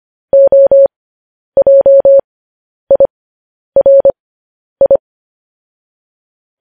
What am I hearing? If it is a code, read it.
Morse code, heard as a keyed tone, about 25 wpm, spaced as for 10 wpm, 550 Hz. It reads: OJIRI
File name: podcast / News headlines-1x-25x10